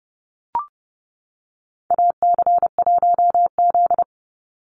Morse code, heard as a keyed tone, about 30 wpm, 700 Hz. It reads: AC1Z
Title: NA SOTA Activator Callsigns 30wpm